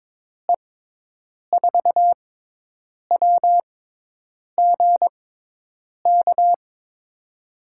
Morse code, heard as a keyed tone, about 22 wpm, spaced as for 8 wpm, 700 Hz. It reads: E4WGK
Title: USA Call Signs - 6 Characters - 6x Character Spacing - 22wpm